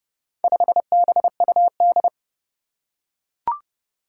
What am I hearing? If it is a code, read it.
Morse code, heard as a keyed tone, about 30 wpm, 700 Hz. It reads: 5BUD
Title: Four Challenging Letter-Number Combinations 30wpm